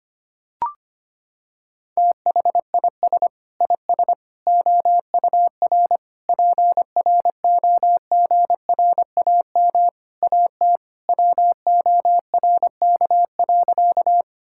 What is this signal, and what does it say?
Morse code, heard as a keyed tone, about 25 wpm, 700 Hz. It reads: THIS IS OUR PROGRAM AT WORK.
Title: Sentences from Top 1000 Words - Review Part 5 - 25wpm